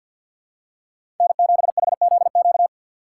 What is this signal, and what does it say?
Morse code, heard as a keyed tone, about 52 wpm, 700 Hz. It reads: N7HZQ
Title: USA Call Signs - 5 Characters - 6x Character Spacing - 50wpm